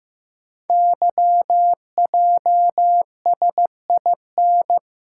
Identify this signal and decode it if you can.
Morse code, heard as a keyed tone, about 15 wpm, 690 Hz. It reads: YJSIN
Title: Code Groups - 5 Letters - 15wpm